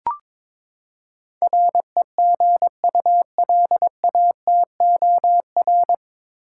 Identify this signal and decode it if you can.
Morse code, heard as a keyed tone, about 22 wpm, 700 Hz. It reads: REGULATOR
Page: Morse Code Ninja practice 22wpm